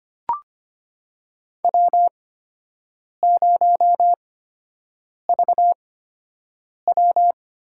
Morse code, heard as a keyed tone, about 25 wpm, spaced as for 7 wpm, 700 Hz. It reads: W0VW
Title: USA Call Signs - 8x Character Spacing - 25wpm